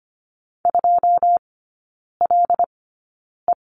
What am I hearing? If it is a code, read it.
Morse code, heard as a keyed tone, about 25 wpm, spaced as for 9 wpm, 700 Hz. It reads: 2LE
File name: Lesson 17e - Letter u - Three Characters 25wpm